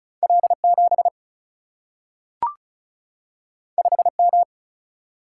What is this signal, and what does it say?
Morse code, heard as a keyed tone, about 36 wpm, 700 Hz. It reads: L7 5M
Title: Single-Letter Single-Number 35wpm